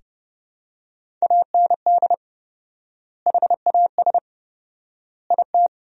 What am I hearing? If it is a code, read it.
Morse code, heard as a keyed tone, about 30 wpm, 700 Hz. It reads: AND HAS IT
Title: 4x Word-Spacing - Sentences from Top 100 Words - 30wpm